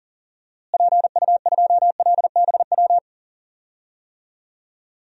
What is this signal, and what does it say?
Morse code, heard as a keyed tone, about 40 wpm, 700 Hz. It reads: PU2LBW
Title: DXCC-100 Call Signs - 6 Characters - Repeated 3x - 40wpm